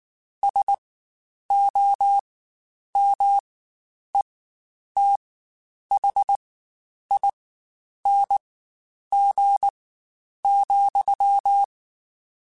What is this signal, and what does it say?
Morse code, heard as a keyed tone, about 19 wpm, spaced as for 9 wpm, 800 Hz. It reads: SOMETHING,